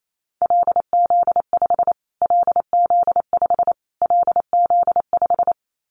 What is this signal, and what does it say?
Morse code, heard as a keyed tone, about 28 wpm, 700 Hz. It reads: LZ5 LZ5 LZ5
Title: Callsign Prefixes for Top 25 DXCC - Repeated 3x - 28wpm